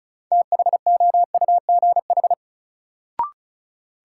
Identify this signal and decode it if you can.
Morse code, heard as a keyed tone, about 34 wpm, 700 Hz. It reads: THOUGH
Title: Sentences from Top 400 Words - New Word Drill - 35wpm